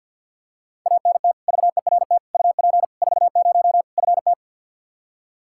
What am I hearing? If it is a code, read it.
Morse code, heard as a keyed tone, about 52 wpm, 700 Hz. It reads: ANT VERT UP 40 FT
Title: Practice QSO - Exchange 2 - Info - 50wpm